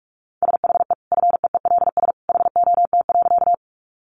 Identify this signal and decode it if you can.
Morse code, heard as a keyed tone, about 44 wpm, 700 Hz. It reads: SHE FEELS HOT.
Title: Additional Easy Phrases 45wpm